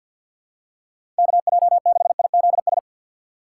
Morse code, heard as a keyed tone, about 52 wpm, 700 Hz. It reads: KJ6IZS
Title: USA Call Signs - 4x Character Spacing - 50wpm